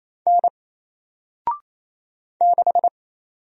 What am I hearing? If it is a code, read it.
Morse code, heard as a keyed tone, about 28 wpm, 700 Hz. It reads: N 6
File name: Single-Letter-Number-28wpm